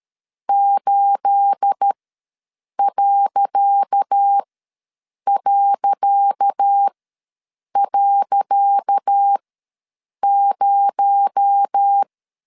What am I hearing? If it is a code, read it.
Morse code, heard as a keyed tone, about 13 wpm, spaced as for 7 wpm, 800 Hz. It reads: 8...0
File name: LEZIONI / 10 SIGNS / 10C DOT ws 12wpm 3548